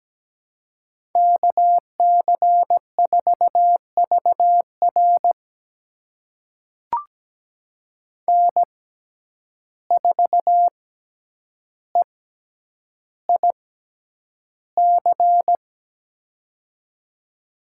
Morse code, heard as a keyed tone, about 17 wpm, spaced as for 9 wpm, 700 Hz. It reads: KC4VR N4EIC
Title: USA Call Signs - 5 Characters - 6x Character Spacing - 17wpm